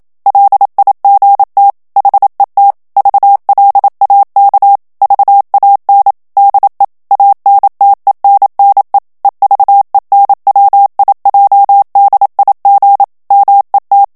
Morse code, heard as a keyed tone, about 27 wpm, 800 Hz. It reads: LIGT HET VLAK VAN DE ANTENNE EVENWIJDIG MET
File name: HOLLANDAIS texte clair 1 vitesse 26 mots minute